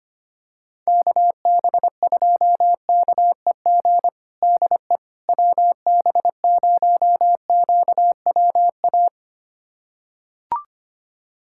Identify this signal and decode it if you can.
Morse code, heard as a keyed tone, about 25 wpm, 700 Hz. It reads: KB2KEG DE WB0QWA
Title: USA Callsign DE Callsign 25wpm